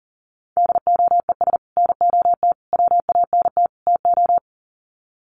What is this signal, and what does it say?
Morse code, heard as a keyed tone, about 40 wpm, 700 Hz. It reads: DOES NOT WANT TO
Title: Sets of 4 Words 40wpm